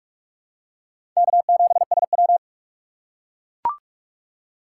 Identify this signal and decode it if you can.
Morse code, heard as a keyed tone, about 41 wpm, 700 Hz. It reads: K7SW
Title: USA Call Signs - 4 Characters - 2x Character Spacing - 45wpm